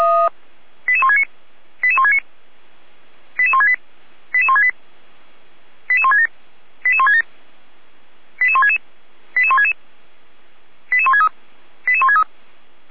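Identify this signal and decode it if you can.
Five-tone selective call (ZVEI), heard as quick runs of five tones, tones of about 70 ms. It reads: 88169 88169 88168 88168 88167 88167 88166 88166 88163 88163
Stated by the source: ZVEI five-tone call received by SDR.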